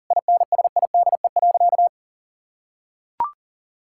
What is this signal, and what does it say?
Morse code, heard as a keyed tone, about 40 wpm, 700 Hz. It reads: INSIDE.
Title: Sentences from Top 1000 Words - Review Part 6 - 40wpm